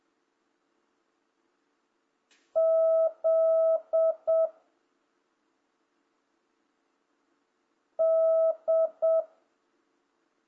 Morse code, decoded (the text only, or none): Z D